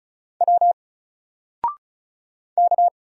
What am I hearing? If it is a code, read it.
Morse code, heard as a keyed tone, about 36 wpm, 700 Hz. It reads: W K
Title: Single Character (Letter:Number:Punctuation:Prosign) Alt 35wpm